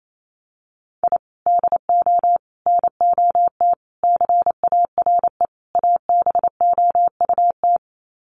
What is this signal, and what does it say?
Morse code, heard as a keyed tone, about 28 wpm, 700 Hz. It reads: I DO NOT CARE ABOUT
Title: Sets of 5 Words Encore2 28wpm